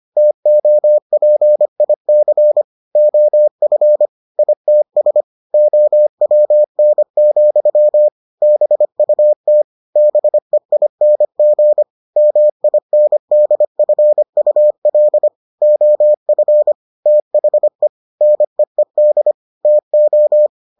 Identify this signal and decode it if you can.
Morse code, heard as a keyed tone, about 25 wpm, 600 Hz. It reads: TOPIC OF ITS OWN, BUT BEING MINDFUL OF THE NEED TO